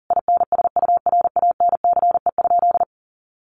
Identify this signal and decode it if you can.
Morse code, heard as a keyed tone, about 40 wpm, 700 Hz. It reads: INSURANCE?
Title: Sentences from Top 1100 Words 40wpm